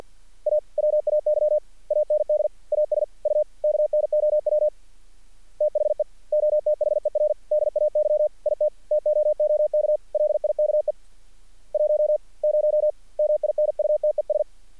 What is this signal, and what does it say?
Morse code, heard as a keyed tone, about 50 wpm, 600 Hz. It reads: AWAY AND AS U KNOW THE OTHER DAY IT TOOK LIKE 10 MINUTES
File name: archived / CRSnet